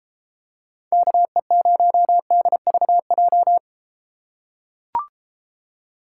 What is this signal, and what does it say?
Morse code, heard as a keyed tone, about 31 wpm, 700 Hz. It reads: KE0DVJ